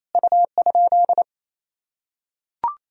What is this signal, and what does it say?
Morse code, heard as a keyed tone, about 28 wpm, 700 Hz. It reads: U?
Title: Sentences from Top 200 Words and Abbreviations 28wpm